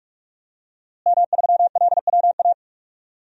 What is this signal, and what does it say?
Morse code, heard as a keyed tone, about 45 wpm, 700 Hz. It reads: M3LWA
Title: DXCC-100 Call Signs - 5 Characters - Repeated 3x - 45wpm